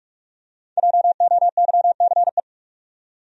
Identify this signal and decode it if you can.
Morse code, heard as a keyed tone, about 44 wpm, 700 Hz. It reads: JOYCE
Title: USA Common Names - Max 5 Letters - 45wpm